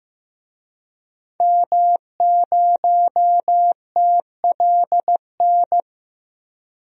Morse code, heard as a keyed tone, about 16 wpm, 700 Hz. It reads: M0TLN
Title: DXCC-100 Call Signs - 5 Characters - 15wpm